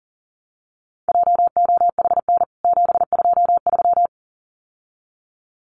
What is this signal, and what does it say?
Morse code, heard as a keyed tone, about 40 wpm, 700 Hz. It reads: JOHN 723
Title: CWT Exchange 40wpm